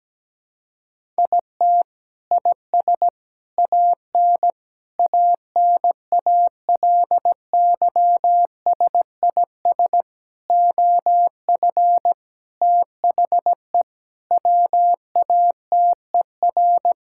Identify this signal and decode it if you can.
Morse code, heard as a keyed tone, about 17 wpm, 700 Hz. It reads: IT IS AN ANALYSIS OF THE WATER
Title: Sentences from Top 1000 Words - Review Part 5 - 17wpm